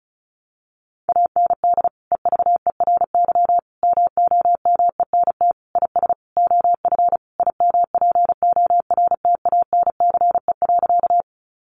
Morse code, heard as a keyed tone, about 34 wpm, 700 Hz. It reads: AND EVERY MOMENT IS OF IMPORTANCE.
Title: Sentences from Top 1200 Words 35wpm